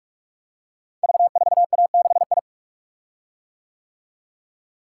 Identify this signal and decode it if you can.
Morse code, heard as a keyed tone, about 44 wpm, 700 Hz. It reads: V4A6I
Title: Code Groups - 5 Letter:Numbers - 45wpm